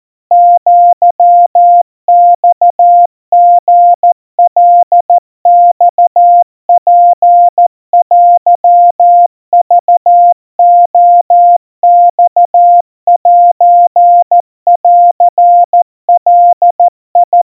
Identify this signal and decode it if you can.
Morse code (Japanese Wabun code), heard as a keyed tone, about 14 wpm, 700 Hz. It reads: アマリカマツテクレマセンガ
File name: kawa / Morse sample / kamyu 50 cpm 700 Hz Crispy